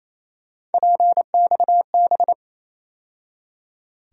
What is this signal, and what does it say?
Morse code, heard as a keyed tone, about 28 wpm, 700 Hz. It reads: PXB
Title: USA Call Sign Postfixes - 3 Letters 28wpm